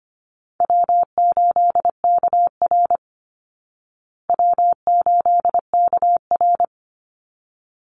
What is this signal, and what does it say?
Morse code, heard as a keyed tone, about 25 wpm, 700 Hz. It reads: W8KR W8KR
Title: Call Signs Repeated 3x - 25wpm